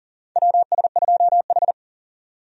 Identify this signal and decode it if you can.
Morse code, heard as a keyed tone, about 40 wpm, 700 Hz. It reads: WS2H
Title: USA Call Signs - 4 Characters - 40wpm